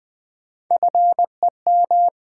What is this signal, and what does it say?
Morse code, heard as a keyed tone, about 20 wpm, 700 Hz. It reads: FEM
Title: Build A Word - 3 Letters At A Time - 8-Letter Words - 20wpm